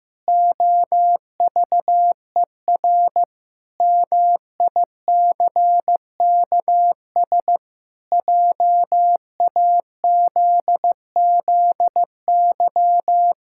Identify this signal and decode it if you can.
Morse code, heard as a keyed tone, about 15 wpm, 700 Hz. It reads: OVER MICKS JAZZY
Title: Pangram Challenge v2 - 15wpm